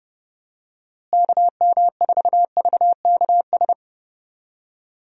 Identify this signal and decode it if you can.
Morse code, heard as a keyed tone, about 30 wpm, 700 Hz. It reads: KM4VKS
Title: USA Call Signs - 6 Characters - Repeated 2x - 30wpm